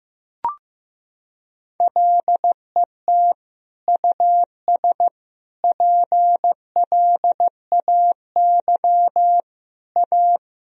Morse code, heard as a keyed tone, about 15 wpm, 700 Hz. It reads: LET US PLAY A
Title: Sentences from Top 1000 Words - Review Part 3 - 15wpm